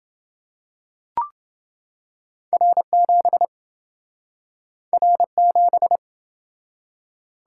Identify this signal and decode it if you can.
Morse code, heard as a keyed tone, about 27 wpm, 700 Hz. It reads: R7 R7